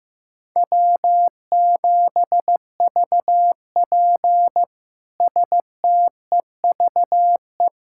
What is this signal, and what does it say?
Morse code, heard as a keyed tone, about 15 wpm, 700 Hz. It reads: W7VP STEVE
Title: K1USN SST QSO Practice - 15wpm